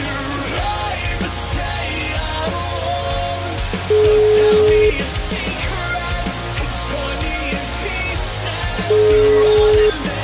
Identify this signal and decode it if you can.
Ringback tone (Europe, 425 Hz), with noise.